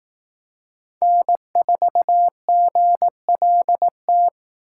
Morse code, heard as a keyed tone, about 18 wpm, 700 Hz. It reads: N4GLT
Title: Speed Racing - USA 1x3 Call Signs 15wpm